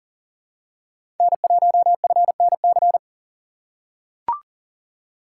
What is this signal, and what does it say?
Morse code, heard as a keyed tone, about 40 wpm, 700 Hz. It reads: N1FNC
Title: USA Call Signs - 5 Characters - 4x Character Spacing - 40wpm